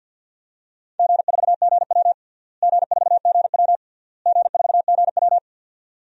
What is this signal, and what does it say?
Morse code, heard as a keyed tone, about 51 wpm, 700 Hz. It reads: G4GW G4GW G4GW
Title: DXCC-100 Call Signs - 4 Characters - Repeated 3x - 50wpm